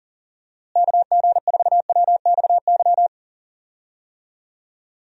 Morse code, heard as a keyed tone, about 37 wpm, 700 Hz. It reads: KG4WXY
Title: USA Call Signs - 6 Characters - Repeated 3x - 40wpm